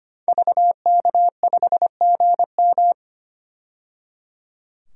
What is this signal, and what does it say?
Morse code, heard as a keyed tone, about 25 wpm, 700 Hz. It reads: VK5GM
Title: DXCC-100 Call Signs - 5 Characters - 25wpm